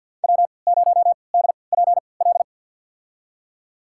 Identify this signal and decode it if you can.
Morse code, heard as a keyed tone, about 51 wpm, spaced as for 28 wpm, 700 Hz. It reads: W0DPL
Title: USA Call Signs - 5 Characters - 3x Character Spacing - 50wpm